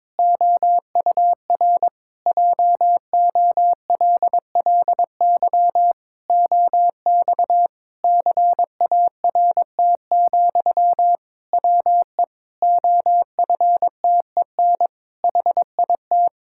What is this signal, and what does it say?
Morse code, heard as a keyed tone, about 21 wpm, 700 Hz. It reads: OUR JOLLY OX CART, WE OFTEN HIT
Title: Pangram Challenge v2 - 22wpm